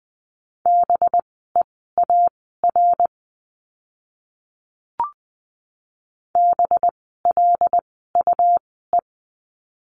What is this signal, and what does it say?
Morse code, heard as a keyed tone, about 20 wpm, spaced as for 15 wpm, 700 Hz. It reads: BEAR BLUE
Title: 185 Lesson 30f - Letter b - Words 20wpm